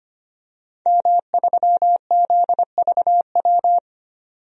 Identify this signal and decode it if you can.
Morse code, heard as a keyed tone, about 25 wpm, 700 Hz. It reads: M3ZVW
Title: DXCC-100 Call Signs - 5 Characters - 25wpm